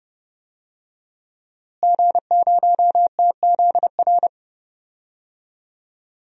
Morse code, heard as a keyed tone, about 31 wpm, 700 Hz. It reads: G0TZR
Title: DXCC-100 Call Signs - 5 Characters - Repeated 2x - 30wpm